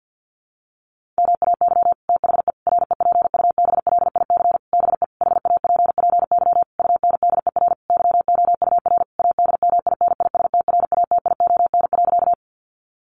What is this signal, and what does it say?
Morse code, heard as a keyed tone, about 50 wpm, 700 Hz. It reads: MAY THE REPUBLIC BE HAPPY UNDER YOUR ADMINISTRATION.
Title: Sentences from Top 1500 Words 50wpm